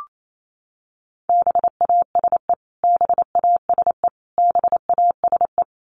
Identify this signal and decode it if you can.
Morse code, heard as a keyed tone, about 28 wpm, 700 Hz. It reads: BASE BASE BASE
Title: Top 500 Words Repeated 28wpm